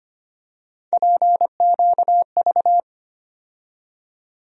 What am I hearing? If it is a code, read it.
Morse code, heard as a keyed tone, about 25 wpm, 700 Hz. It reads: PQV
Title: USA Call Sign Postfixes - 3 Letters 25wpm